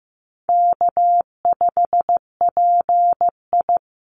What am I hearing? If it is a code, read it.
Morse code, heard as a keyed tone, about 15 wpm, 700 Hz. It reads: K5PI
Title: Easy Call Signs Repeated 2x - 15wpm